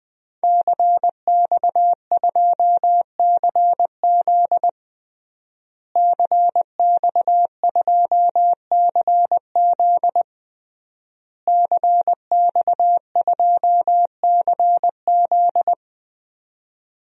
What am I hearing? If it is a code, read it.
Morse code, heard as a keyed tone, about 20 wpm, 700 Hz. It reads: CX2CZ CX2CZ CX2CZ
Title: DXCC-100 Call Signs - 5 Characters - Repeated 3x - 20wpm